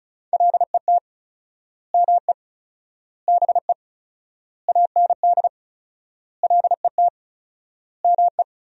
Morse code, heard as a keyed tone, about 36 wpm, 700 Hz. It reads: LET ME BE AND LET ME
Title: 4x Word-Spacing - Sentences from Top 300 Words - 35wpm